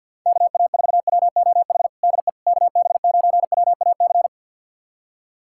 Morse code, heard as a keyed tone, about 50 wpm, 700 Hz. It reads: KA4WOH DE KB9WAC
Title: USA Callsign DE Callsign 50wpm